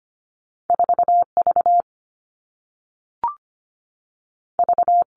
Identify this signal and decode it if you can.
Morse code, heard as a keyed tone, about 25 wpm, 700 Hz. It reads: 4 V V
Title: All Cut Numbers - 2-Digit 25wpm